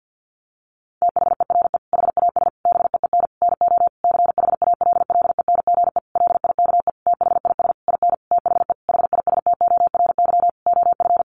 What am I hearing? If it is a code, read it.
Morse code, heard as a keyed tone, about 50 wpm, 700 Hz. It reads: THERE HAS BEEN NO CHALLENGE LIKE THIS IN THE HISTORY OF
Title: Sentences from Top 1500 Words 50wpm